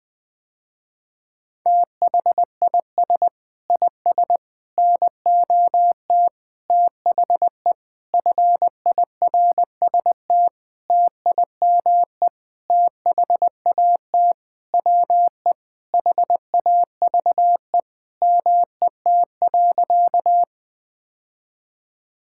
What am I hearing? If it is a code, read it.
Morse code, heard as a keyed tone, about 20 wpm, 700 Hz. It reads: THIS IS NOT THE FIRST TIME THAT WE HAVE MET.
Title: Sentences with 7-Word Phrases - Spoken Beginning - 20wpm